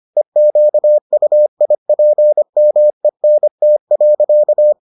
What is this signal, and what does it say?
Morse code, heard as a keyed tone, about 25 wpm, 600 Hz. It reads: EQUIPMENT.